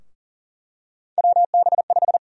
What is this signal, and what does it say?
Morse code, heard as a keyed tone, about 40 wpm, 700 Hz. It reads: WB5
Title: USA Call Sign Prefixes - 40wpm